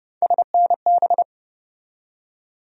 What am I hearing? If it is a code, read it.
Morse code, heard as a keyed tone, about 31 wpm, 700 Hz. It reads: SNB
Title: USA Call Sign Postfixes - 3 Letters 30wpm